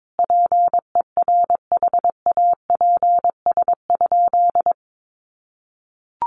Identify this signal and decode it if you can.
Morse code, heard as a keyed tone, about 22 wpm, 700 Hz. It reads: PERHAPS?
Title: Sentences from Top 1200 Words 22wpm